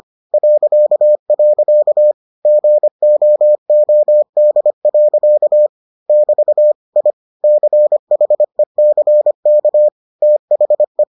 Morse code, heard as a keyed tone, about 25 wpm, 600 Hz. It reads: .. GOOD. <BT> I CHECK THE